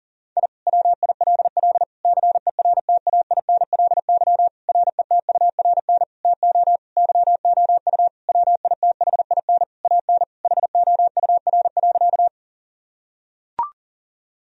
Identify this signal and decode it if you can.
Morse code, heard as a keyed tone, about 40 wpm, 700 Hz. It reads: I WILL CERTAINLY RETURN TO YOU WITHIN AN HOUR.
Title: Sentences from Top 1000 Words - Review Part 4 - 40wpm